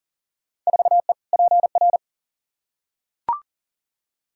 Morse code, heard as a keyed tone, about 40 wpm, 700 Hz. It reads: 4E PR
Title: Field Day Exchanges Encore - 40wpm